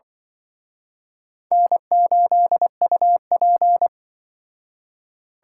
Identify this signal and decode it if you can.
Morse code, heard as a keyed tone, about 25 wpm, 700 Hz. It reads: N8UP
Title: Speed Racing - USA Call Signs 20wpm